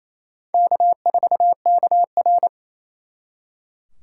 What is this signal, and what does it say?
Morse code, heard as a keyed tone, about 34 wpm, 700 Hz. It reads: K4KR
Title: Speed Racing - USA 1x2 Call Signs 28wpm